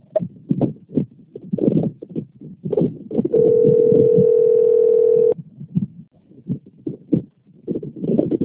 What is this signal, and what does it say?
Ringback tone (North America), 440 + 480 Hz.
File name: rg-1600-unknown- - - .11884